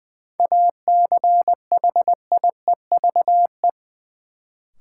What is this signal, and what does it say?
Morse code, heard as a keyed tone, about 20 wpm, 700 Hz. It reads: ACHIEVE
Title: Sentences from Top 1000 Words - Review Part 5 - New Word Drill - 20wpm